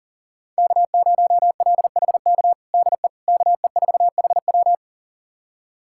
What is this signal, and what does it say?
Morse code, heard as a keyed tone, about 40 wpm, 700 Hz. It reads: K0LHK DE KE4HW
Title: USA Callsign DE Callsign - 5 Characters - 40wpm